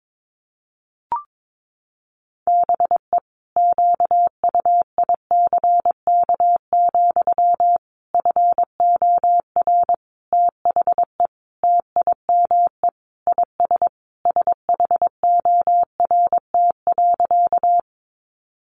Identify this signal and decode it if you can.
Morse code, heard as a keyed tone, about 22 wpm, 700 Hz. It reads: BE QUICK, FOR THE TIME IS SHORT.
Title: Sentences from Top 500 Words 22wpm